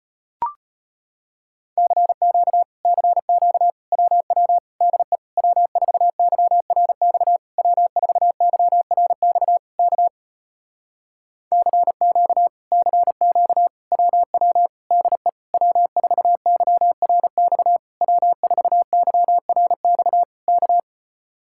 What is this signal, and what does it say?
Morse code, heard as a keyed tone, about 33 wpm, 700 Hz. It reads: CQ CQ WW DE W4YRX W4YRX K CQ CQ WW DE W4YRX W4YRX K